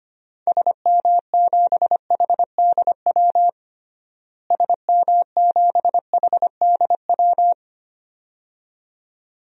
Morse code, heard as a keyed tone, about 25 wpm, 700 Hz. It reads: SM7HDW SM7HDW
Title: DXCC-100 Call Signs - 6 Characters - Repeated 2x - 25wpm